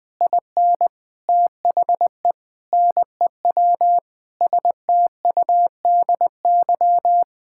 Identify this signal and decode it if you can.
Morse code, heard as a keyed tone, about 20 wpm, 700 Hz. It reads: IN THE NEW STUDY
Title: Sets of 8 Words - 20wpm